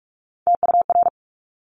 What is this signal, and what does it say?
Morse code, heard as a keyed tone, about 46 wpm, 690 Hz. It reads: TUR
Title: Top Three Letter Combinations in Words 45wpm